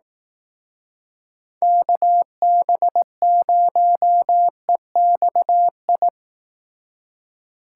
Morse code, heard as a keyed tone, about 18 wpm, 700 Hz. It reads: KB0EXI
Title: Speed Racing - USA 2x3 Call Signs 15wpm